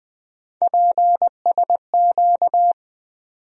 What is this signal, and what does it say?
Morse code, heard as a keyed tone, about 20 wpm, 700 Hz. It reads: PSQ
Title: USA Call Sign Postfixes - 3 Letters 20wpm